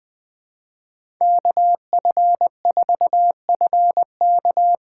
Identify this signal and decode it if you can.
Morse code, heard as a keyed tone, about 20 wpm, 700 Hz. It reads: KF4FK
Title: USA Call Signs - 6 Characters - 6x Character Spacing - 20wpm